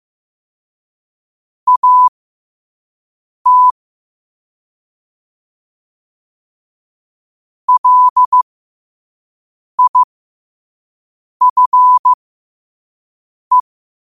Morse code, heard as a keyed tone, about 15 wpm, spaced as for 5 wpm, 1000 Hz. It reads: AT LIFE
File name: 2024-10-07-5wpm